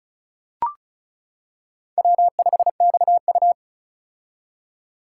Morse code, heard as a keyed tone, about 36 wpm, 700 Hz. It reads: W5XU
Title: K1USN SST QSO Practice - 35wpm